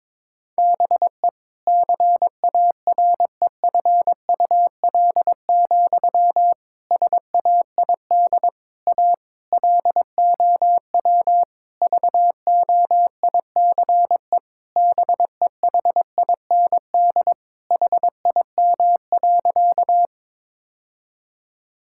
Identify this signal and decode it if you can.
Morse code, heard as a keyed tone, about 22 wpm, 700 Hz. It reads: BE CAREFUL, SAID A LOW VOICE BEHIND HIM.
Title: Sentences from Top 1600 Words 22wpm